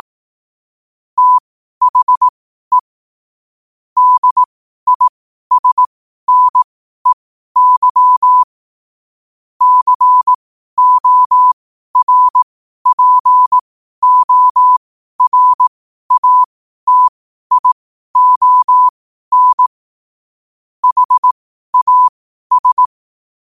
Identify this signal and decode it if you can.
Morse code, heard as a keyed tone, about 18 wpm, spaced as for 12 wpm, 1000 Hz. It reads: THE DISNEY CORPORATION HAS
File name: qotdfiles / 12wpm